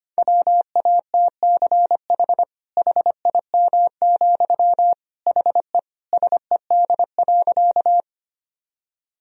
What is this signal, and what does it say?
Morse code, heard as a keyed tone, about 25 wpm, 700 Hz. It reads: WATCH HIM, HE SED.
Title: Sentences from Top 300 Words and Abbreviations 25wpm